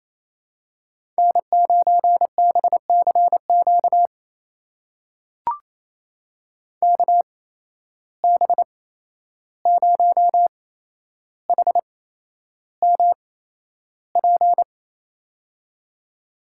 Morse code, heard as a keyed tone, about 28 wpm, spaced as for 11 wpm, 700 Hz. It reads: N9BCQ KB0HMP
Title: USA Call Signs - 8x Character Spacing - 28wpm